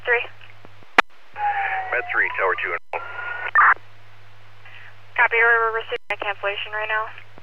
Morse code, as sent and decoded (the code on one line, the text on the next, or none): none